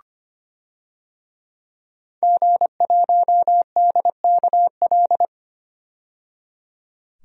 Morse code, heard as a keyed tone, about 25 wpm, 700 Hz. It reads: G1DKL
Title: DXCC-100 Call Signs - 5 Characters - 25wpm